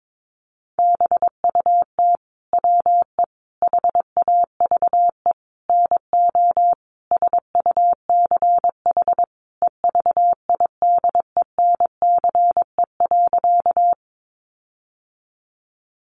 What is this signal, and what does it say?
Morse code, heard as a keyed tone, about 22 wpm, 700 Hz. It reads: BUT WE HAVE NO SUCH EVIDENCE.
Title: Sentences from Top 1300 Words 22wpm